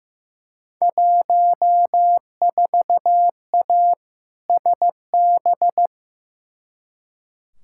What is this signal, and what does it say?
Morse code, heard as a keyed tone, about 15 wpm, 700 Hz. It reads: 14A SB
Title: Field Day Exchanges Encore - 15wpm